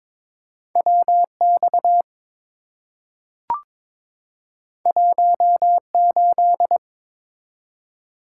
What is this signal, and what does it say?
Morse code, heard as a keyed tone, about 22 wpm, 700 Hz. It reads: WX 18
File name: practice / 22wpm / Two Letter-Number Combinations - 22wpm